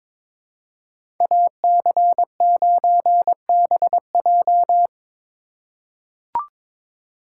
Morse code, heard as a keyed tone, about 21 wpm, 700 Hz. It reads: AC9BJ
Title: USA Call Signs - 5 Characters - 3x Character Spacing - 22wpm